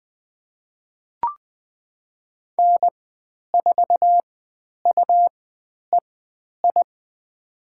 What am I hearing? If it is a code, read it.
Morse code, heard as a keyed tone, about 20 wpm, spaced as for 10 wpm, 700 Hz. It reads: N4UEI
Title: 109 Lesson 18g - Letter m - Call Signs 20wpm